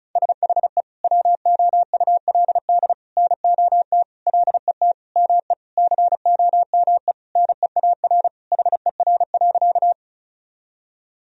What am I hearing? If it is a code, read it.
Morse code, heard as a keyed tone, about 34 wpm, 700 Hz. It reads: SHE WOULD NOT LET ME COME NEAR HER.
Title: Sentences from Top 1000 Words - Review Part 1 - 35wpm